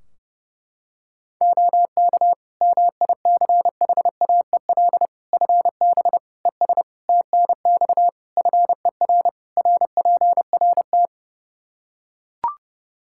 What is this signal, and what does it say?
Morse code, heard as a keyed tone, about 30 wpm, 700 Hz. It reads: OK MICHAEL FB ES TNX FER RPRT
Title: K1USN SST QSO Practice 30wpm